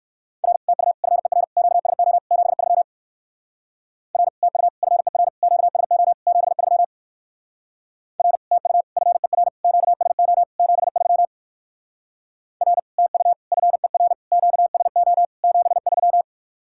Morse code, heard as a keyed tone, about 59 wpm, 700 Hz. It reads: R TU FER QSO 73 R TU FER QSO 73 R TU FER QSO 73 R TU FER QSO 73